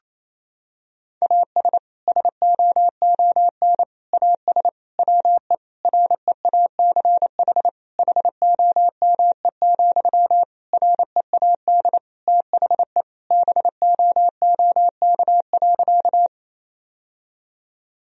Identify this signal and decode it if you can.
Morse code, heard as a keyed tone, about 28 wpm, 700 Hz. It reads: AS SOON AS WE REACH HOME, READ THE BOOK.
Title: Sentences from Top 1000 Words - Review Part 2 - 28wpm